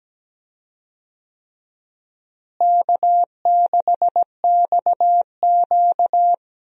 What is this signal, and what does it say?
Morse code, heard as a keyed tone, about 17 wpm, 700 Hz. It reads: K6XQ
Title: USA Call Signs - 4 Characters - Repeated 3x - 17wpm